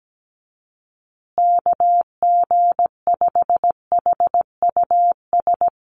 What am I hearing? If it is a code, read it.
Morse code, heard as a keyed tone, about 17 wpm, 700 Hz. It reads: KG5HUS
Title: USA Call Signs Repeated 3x 17wpm